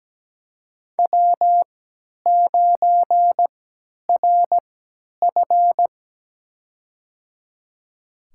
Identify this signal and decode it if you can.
Morse code, heard as a keyed tone, about 17 wpm, spaced as for 10 wpm, 700 Hz. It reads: W9RF
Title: USA Call Signs - 4 Characters - 3x Character Spacing - 17wpm